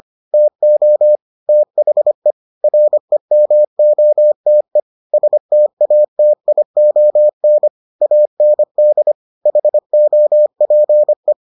Morse code, heard as a keyed tone, about 25 wpm, 600 Hz. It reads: TO THE REMOTE STATION AND HOPE